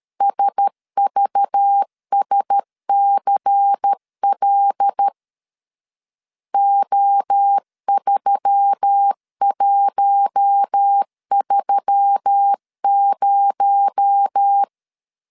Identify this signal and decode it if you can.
Morse code, heard as a keyed tone, about 13 wpm, 800 Hz. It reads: SVSCL O3130
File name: CW / DET / 1 6 1500C S B 12wpm 3431